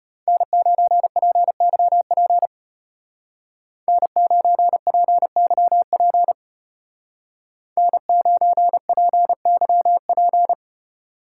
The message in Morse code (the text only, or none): N9PYP N9PYP N9PYP